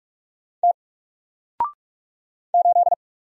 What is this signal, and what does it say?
Morse code, heard as a keyed tone, about 48 wpm, 700 Hz. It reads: T 8
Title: Single Letter-Number Alt 45wpm